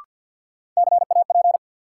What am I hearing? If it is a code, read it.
Morse code, heard as a keyed tone, about 51 wpm, 700 Hz. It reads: CAP
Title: Sentences from Top 1600 Words - New Word Drill Alt - 50wpm